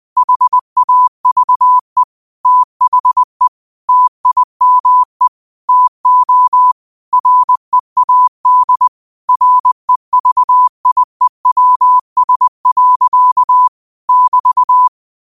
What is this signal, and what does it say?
Morse code, heard as a keyed tone, about 20 wpm, 1000 Hz. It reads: HAVE THE TIME TO READ REVIEWS. <BT>
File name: qotdfiles / 20wpm